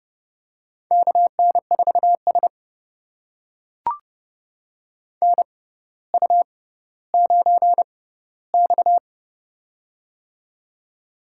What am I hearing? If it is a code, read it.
Morse code, heard as a keyed tone, about 30 wpm, spaced as for 15 wpm, 700 Hz. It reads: KN4S NU9X
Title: USA Call Signs - 4 Characters - 6x Character Spacing - 30wpm